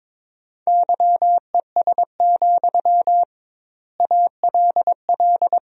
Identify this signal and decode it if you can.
Morse code, heard as a keyed tone, about 22 wpm, 700 Hz. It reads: YES, ALL
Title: 2x Word-Spacing - Sentences from Top 500 Words - 22wpm